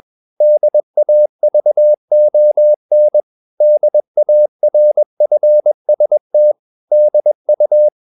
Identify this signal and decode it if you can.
Morse code, heard as a keyed tone, about 21 wpm, 600 Hz. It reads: DAVON DARFST DU
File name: marienkind2 21wpm